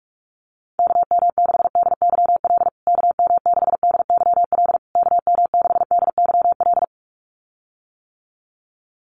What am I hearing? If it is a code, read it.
Morse code, heard as a keyed tone, about 44 wpm, 700 Hz. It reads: KM6DYL KM6DYL KM6DYL
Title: USA Call Signs Repeated 3x 45wpm